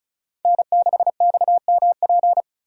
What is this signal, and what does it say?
Morse code, heard as a keyed tone, about 36 wpm, 700 Hz. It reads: N6XMP
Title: USA Call Signs - 4x Character Spacing - 35wpm